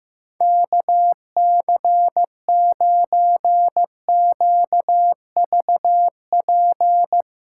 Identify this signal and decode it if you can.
Morse code, heard as a keyed tone, about 15 wpm, 700 Hz. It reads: KC9QVP
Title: USA Callsign DE Callsign - 6 Characters - 15wpm